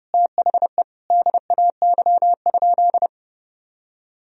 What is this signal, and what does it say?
Morse code, heard as a keyed tone, about 30 wpm, 700 Hz. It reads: THE DAY?
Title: Sentences from Top 1000 Words - Review Part 4 - 30wpm